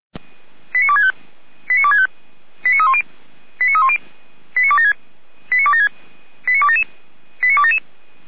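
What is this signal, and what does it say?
Five-tone selective call (ZVEI), heard as quick runs of five tones, tones of about 70 ms. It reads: 89265 89265 89310 89310 89276 89276 89277 89277
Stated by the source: Five-tone ZVEI selective call 89250 received by SDR, archived by Feuerwehr Orlamünde (fire brigade).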